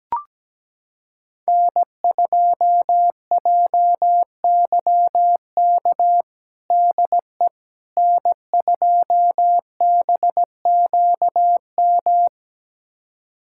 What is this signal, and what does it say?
Morse code, heard as a keyed tone, about 17 wpm, 700 Hz. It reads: N2JYK DE N2BQM
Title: USA Callsign DE Callsign - 5 Characters - 17wpm